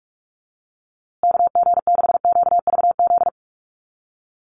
Morse code, heard as a keyed tone, about 44 wpm, 700 Hz. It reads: KG6QVZ
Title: USA Call Signs Repeated 3x 45wpm